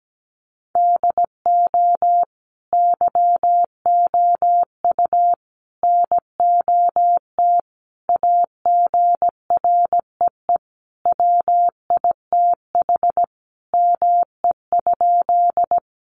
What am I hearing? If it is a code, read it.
Morse code, heard as a keyed tone, about 17 wpm, 700 Hz. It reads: DO YOU NOT AGREE WITH ME?
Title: Sentences from Top 1100 Words 17wpm